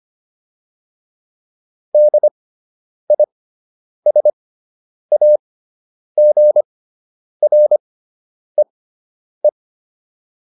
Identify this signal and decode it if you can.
Morse code, heard as a keyed tone, about 25 wpm, spaced as for 9 wpm, 600 Hz. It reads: DISAGREE